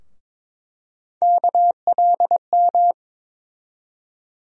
Morse code, heard as a keyed tone, about 22 wpm, 700 Hz. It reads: KLM
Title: USA Call Sign Postfixes - 3 Letters 22wpm